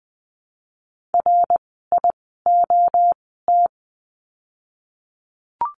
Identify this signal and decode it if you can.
Morse code, heard as a keyed tone, about 20 wpm, 700 Hz. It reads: R I O T
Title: 061 Lesson 11f - Letter h - Words 20wpm